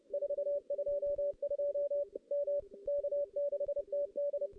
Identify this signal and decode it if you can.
Morse code, heard as a keyed tone, about 30 wpm, 570 Hz. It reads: 422 M K6TD
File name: contests / CW-SS-k5tr- -1500